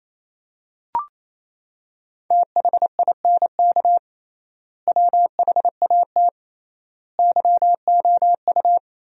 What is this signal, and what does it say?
Morse code, heard as a keyed tone, about 28 wpm, 700 Hz. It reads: THINK WHAT YOU
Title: 3x Word-Spacing - Sentences from Top 200 Words - 28wpm